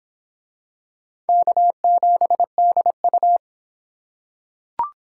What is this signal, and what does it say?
Morse code, heard as a keyed tone, about 26 wpm, 700 Hz. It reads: K7DU